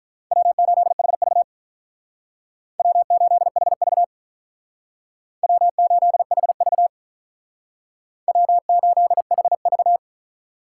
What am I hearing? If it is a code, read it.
Morse code, heard as a keyed tone, about 46 wpm, 700 Hz. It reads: W8HV W8HV W8HV W8HV